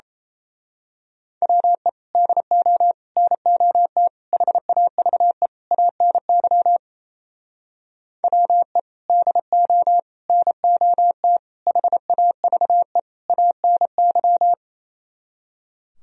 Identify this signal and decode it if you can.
Morse code, heard as a keyed tone, about 31 wpm, 700 Hz. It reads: WE DO NOT HAVE ANY WE DO NOT HAVE ANY
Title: Speed-Racing - Sets of 5 Words 28wpm